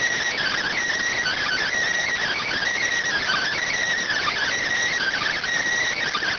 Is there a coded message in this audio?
Yes: Emergency Alert System (SAME) tones (header bursts) from 1.2 to 5.4 s, heard as harsh buzzing bursts.